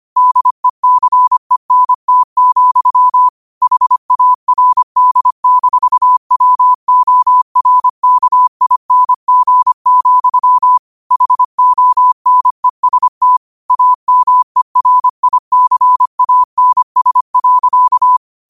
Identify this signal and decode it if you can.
Morse code, heard as a keyed tone, about 25 wpm, 1000 Hz. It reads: DECENT, HARD-WORKING, HONEST AMERICANS.